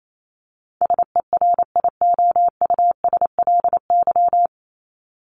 Morse code, heard as a keyed tone, about 28 wpm, 700 Hz. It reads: SERIOUSLY
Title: LY Suffix - Top 100 Words with Base Words 28wpm